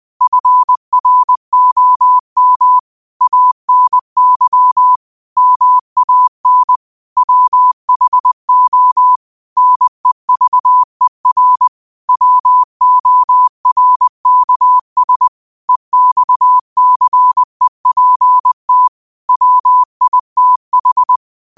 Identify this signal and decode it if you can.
Morse code, heard as a keyed tone, about 20 wpm, 1000 Hz. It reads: FROM ANY MAN WHO NEVER WORKS EXCEPT WITH
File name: qotdfiles / 20wpm